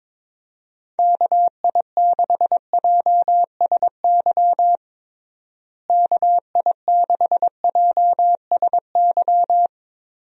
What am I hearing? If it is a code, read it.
Morse code, heard as a keyed tone, about 22 wpm, 700 Hz. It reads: KI6JSY KI6JSY
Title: USA Call Signs - 6 Characters - Repeated 2x - 22wpm